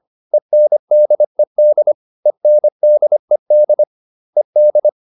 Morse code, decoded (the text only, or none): ENDED ENDED ED